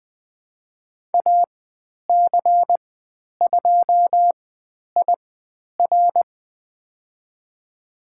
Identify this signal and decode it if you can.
Morse code, heard as a keyed tone, about 20 wpm, spaced as for 10 wpm, 700 Hz. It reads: AC2IR
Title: 095 Lesson 16g - Letter c - Call Signs 20wpm